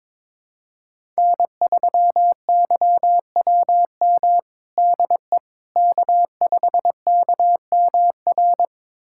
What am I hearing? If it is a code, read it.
Morse code, heard as a keyed tone, about 22 wpm, 700 Hz. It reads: N3YWM DE K5KMR
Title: USA Callsign DE Callsign - 5 Characters - 22wpm